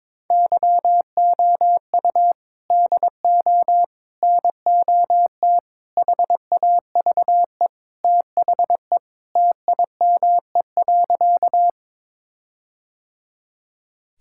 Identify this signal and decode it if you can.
Morse code, heard as a keyed tone, about 22 wpm, 700 Hz. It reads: YOU DO NOT HAVE THE TIME.
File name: practice / 22wpm / Sentences with 4-Word Phrases - 22wpm